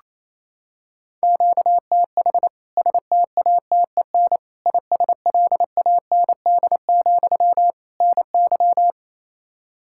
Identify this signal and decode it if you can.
Morse code, heard as a keyed tone, about 27 wpm, 700 Hz. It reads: QTH STATEN ISLAND, NY
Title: K1USN SST QSO Practice 28wpm